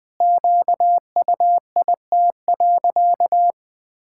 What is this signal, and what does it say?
Morse code, heard as a keyed tone, about 19 wpm, 700 Hz. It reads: QUIT.
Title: Pangram Challenge v2 - 20wpm